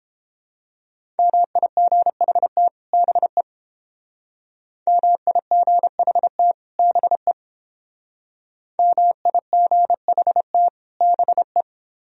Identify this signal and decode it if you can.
Morse code, heard as a keyed tone, about 29 wpm, 700 Hz. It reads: MIGHT BE MIGHT BE MIGHT BE